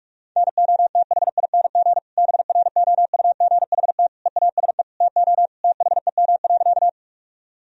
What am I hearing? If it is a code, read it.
Morse code, heard as a keyed tone, about 46 wpm, 700 Hz. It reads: NOTHING BROUGHT EASE TO THEM.
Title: Sentences from Top 1000 Words - Review Part 1 - 45wpm